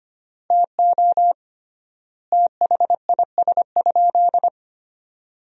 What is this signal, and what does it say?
Morse code, heard as a keyed tone, about 25 wpm, 700 Hz. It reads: TO THIS?
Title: 3x Word-Spacing - Sentences from Top 100 Words - 25wpm